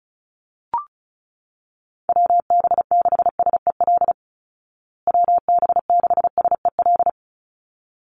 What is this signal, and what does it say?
Morse code, heard as a keyed tone, about 34 wpm, 700 Hz. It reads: WB6SEL WB6SEL
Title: Call Signs Repeated 3x - 35wpm